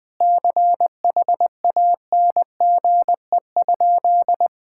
Morse code, heard as a keyed tone, about 21 wpm, 700 Hz. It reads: CHANGE?
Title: Sentences from Top 1600 Words 20wpm